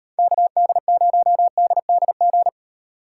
Morse code, heard as a keyed tone, about 31 wpm, 700 Hz. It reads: KD0DDG
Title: Speed Racing - USA 2x3 Call Signs 25wpm